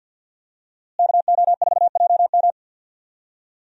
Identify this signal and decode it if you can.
Morse code, heard as a keyed tone, about 49 wpm, 700 Hz. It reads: KO4JM
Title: USA Call Signs - 5 Characters - Repeated 2x - 50wpm